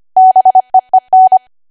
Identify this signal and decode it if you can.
Morse code, heard as a keyed tone, about 25 wpm, 750 Hz. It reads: BEEN
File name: Archive / 25 WPM / 25WPM